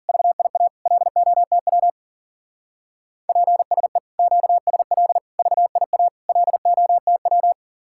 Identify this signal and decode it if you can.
Morse code, heard as a keyed tone, about 51 wpm, 700 Hz. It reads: VIA LOTW PSE QSL VIA LOTW